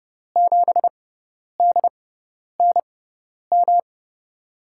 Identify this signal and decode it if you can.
Morse code, heard as a keyed tone, about 30 wpm, spaced as for 10 wpm, 700 Hz. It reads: 7DNM
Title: USA Call Signs - 5 Characters - 6x Character Spacing - 30wpm